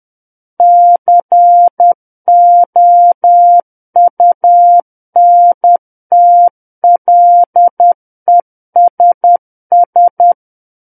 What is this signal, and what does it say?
Morse code, heard as a keyed tone, about 10 wpm, 700 Hz. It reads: COUNTLESS